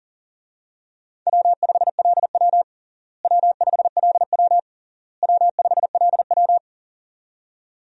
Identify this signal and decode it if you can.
Morse code, heard as a keyed tone, about 40 wpm, 700 Hz. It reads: W5LW W5LW W5LW
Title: USA Call Signs - 4 Characters - Repeated 3x - 40wpm